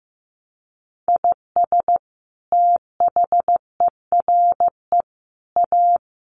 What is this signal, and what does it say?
Morse code, heard as a keyed tone, about 15 wpm, 700 Hz. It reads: IS THERE A
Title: Sentences from Top 800 Words 15wpm